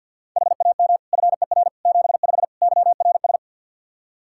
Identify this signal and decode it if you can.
Morse code, heard as a keyed tone, about 53 wpm, 700 Hz. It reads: HAM FER 75 YRS
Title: Practice QSO - Exchange 3 - More Info - 50wpm